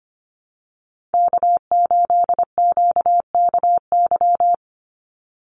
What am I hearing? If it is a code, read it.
Morse code, heard as a keyed tone, about 25 wpm, 700 Hz. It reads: K8QKY
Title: Call Signs - 25wpm